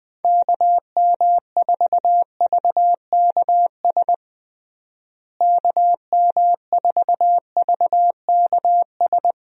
Morse code, heard as a keyed tone, about 20 wpm, 700 Hz. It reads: KM4VKS KM4VKS
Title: USA Call Signs - 6 Characters - Repeated 2x - 20wpm